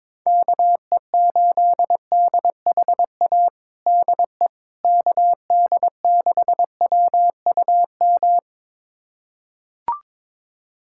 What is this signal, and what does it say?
Morse code, heard as a keyed tone, about 22 wpm, 700 Hz. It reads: KE8DHA DE KD6WUM
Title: USA Callsign DE Callsign - 6 Characters - 22wpm